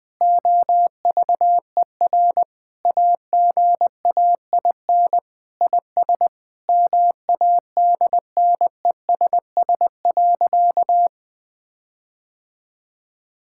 Morse code, heard as a keyed tone, about 21 wpm, 700 Hz. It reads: OVER AGAIN IS MADNESS.
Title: Sentences with 7-Word Phrases - Spoken Beginning - 20wpm